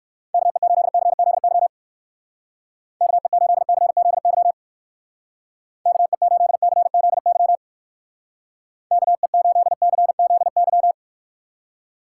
Morse code, heard as a keyed tone, about 54 wpm, 700 Hz. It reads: KE8CZY KE8CZY KE8CZY KE8CZY
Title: Speed Racing - USA 2x3 Call Signs 45wpm